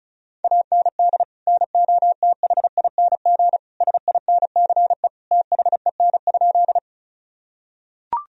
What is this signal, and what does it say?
Morse code, heard as a keyed tone, about 36 wpm, 700 Hz. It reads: AND NOTHING SINCE THEN?
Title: Sentences from Top 1000 Words - Review Part 1 - 35wpm